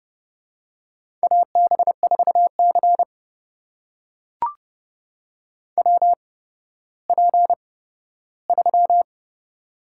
Morse code, heard as a keyed tone, about 30 wpm, spaced as for 15 wpm, 700 Hz. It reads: AB4C WP3
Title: USA Call Signs - 4 Characters - 8x Character Spacing - 30wpm